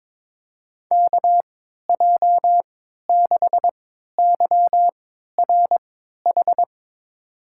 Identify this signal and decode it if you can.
Morse code, heard as a keyed tone, about 22 wpm, spaced as for 12 wpm, 700 Hz. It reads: KJ6YRH
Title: USA Call Signs - 6 Characters - 3x Character Spacing - 22wpm